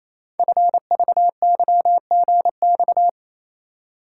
Morse code, heard as a keyed tone, about 28 wpm, 700 Hz. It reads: FVYGX
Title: Code Groups - 5 Letters - 28wpm